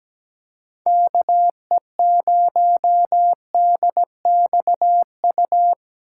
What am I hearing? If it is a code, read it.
Morse code, heard as a keyed tone, about 17 wpm, 700 Hz. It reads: KE0DXU
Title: USA Call Signs - 6 Characters - 4x Character Spacing - 17wpm